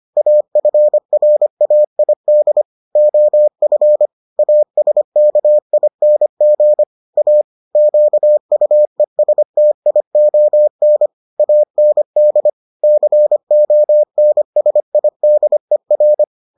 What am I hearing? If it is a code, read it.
Morse code, heard as a keyed tone, about 25 wpm, 600 Hz. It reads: AFRAID OF ASKING A QUESTION AND CONSIDER